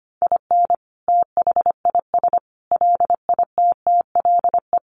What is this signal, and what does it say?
Morse code, heard as a keyed tone, about 25 wpm, 700 Hz. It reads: IN THIS LITTLE
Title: Sentences from Top 1200 Words 25wpm